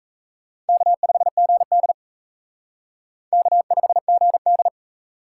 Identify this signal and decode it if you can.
Morse code, heard as a keyed tone, about 34 wpm, 700 Hz. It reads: K5GD K5GD
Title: Speed Racing - USA 1x2 Call Signs 28wpm